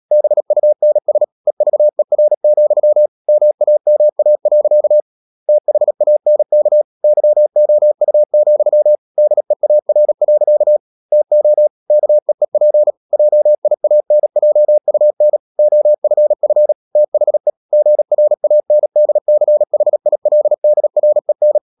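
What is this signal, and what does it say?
Morse code, heard as a keyed tone, about 37 wpm, 600 Hz. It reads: BUNS EVER, MAMA. THANK YOU, DEAR. TO KEEP JIANJUN OFF THE GRANDCHILDREN
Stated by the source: Morse code practice at 35–39 wpm sending English text of Ken Follett's novel Never, from CW med Gnister.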